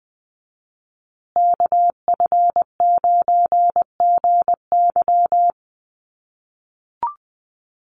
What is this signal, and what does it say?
Morse code, heard as a keyed tone, about 20 wpm, 700 Hz. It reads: KF9GY
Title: USA Call Signs Repeated 2x 20wpm